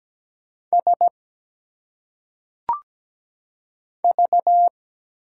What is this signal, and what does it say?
Morse code, heard as a keyed tone, about 17 wpm, 690 Hz. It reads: S V
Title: Single Letters - Phonetic Pronunciation 17wpm